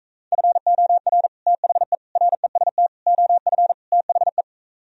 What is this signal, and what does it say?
Morse code, heard as a keyed tone, about 45 wpm, 700 Hz. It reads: FOR THE REST OF THE